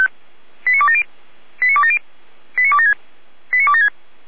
Five-tone selective call (ZVEI), heard as quick runs of five tones, tones of about 70 ms. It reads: 89270 89270 89276 89276